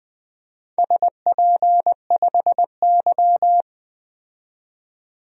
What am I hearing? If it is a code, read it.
Morse code, heard as a keyed tone, about 20 wpm, 700 Hz. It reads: SP5Y
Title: DXCC-100 Call Signs - 4 Characters - 20wpm